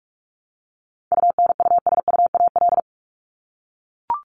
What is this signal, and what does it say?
Morse code, heard as a keyed tone, about 44 wpm, 700 Hz. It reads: UNUSUAL
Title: Top 1251-1500 Words 45wpm